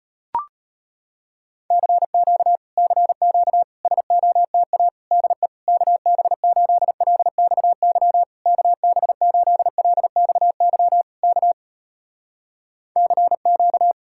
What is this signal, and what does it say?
Morse code, heard as a keyed tone, about 35 wpm, 700 Hz. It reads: CQ CQ SOTA DE KB8LXY KB8LXY K CQ